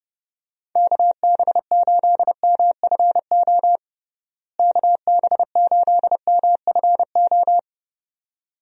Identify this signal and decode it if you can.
Morse code, heard as a keyed tone, about 30 wpm, 700 Hz. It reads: KB8MFO KB8MFO
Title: USA Call Signs - 6 Characters - Repeated 2x - 30wpm